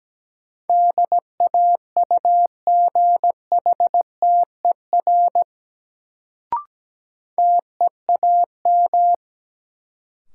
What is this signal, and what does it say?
Morse code, heard as a keyed tone, about 17 wpm, 700 Hz. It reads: DAUGHTER TEAM
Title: Sentences from Top 1500 Words - New Word Drill - 17wpm